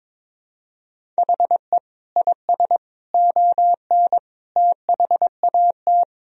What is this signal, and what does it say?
Morse code, heard as a keyed tone, about 22 wpm, 700 Hz. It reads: HE IS ON THAT
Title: Sentences from Top 1000 Words - Review Part 1 - 22wpm